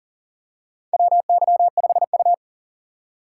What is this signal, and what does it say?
Morse code, heard as a keyed tone, about 40 wpm, 700 Hz. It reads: WY5U
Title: USA Call Signs - 2x Character Spacing - 40wpm